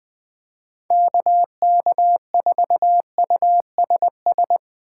Morse code, heard as a keyed tone, about 21 wpm, 700 Hz. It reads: KK4USS